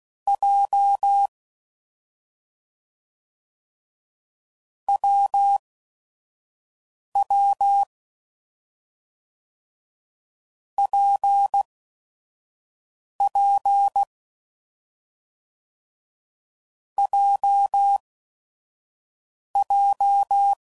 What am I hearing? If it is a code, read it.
Morse code, heard as a keyed tone, about 16 wpm, spaced as for 5 wpm, 800 Hz. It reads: J WW PP JJ
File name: Combinations / Difficult Combos / WPJ WPJ1